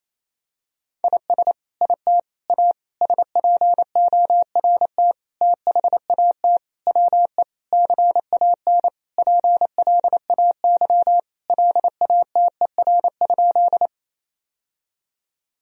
Morse code, heard as a keyed tone, about 28 wpm, 700 Hz. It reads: IS IT A SPORT THAT WE CAN PLAY LATER?
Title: Sentences from Top 1000 Words - Review Part 4 - 28wpm